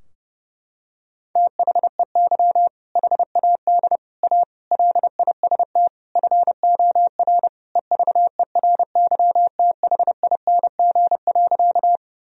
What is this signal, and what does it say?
Morse code, heard as a keyed tone, about 30 wpm, 700 Hz. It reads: THEY HAD A LIST FOR EVERYTHING.
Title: Sentences from Top 1000 Words - Review Part 2 - 30wpm